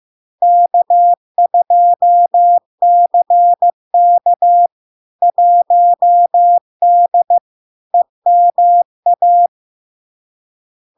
Morse code, heard as a keyed tone, about 15 wpm, 700 Hz. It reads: K2CK 1D EMA
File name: LARK-FD-2022-Log-in-CW-15wpm-700-Hz